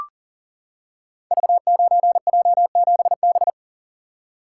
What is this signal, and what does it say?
Morse code, heard as a keyed tone, about 38 wpm, 700 Hz. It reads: V9J7B
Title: Code Groups - 5 Letter:Numbers - 40wpm